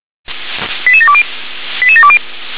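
Five-tone selective call (ZVEI), heard as quick runs of five tones, tones of about 70 ms. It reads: 88720 88720
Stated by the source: ZVEI five-tone selective call received on an SDR.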